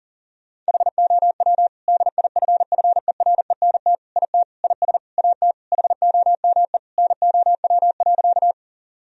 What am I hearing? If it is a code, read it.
Morse code, heard as a keyed tone, about 40 wpm, 700 Hz. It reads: HOW DIFFERENT IT IS AT HOME NOW.
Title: Sentences from Top 1000 Words - Review Part 3 - 40wpm